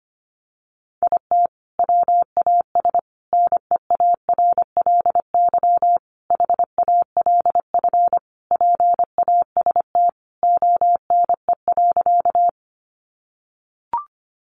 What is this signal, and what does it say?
Morse code, heard as a keyed tone, about 25 wpm, 700 Hz. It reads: IT WAS NEARLY HALF PAST ONE.
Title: Sentences from Top 1300 Words 25wpm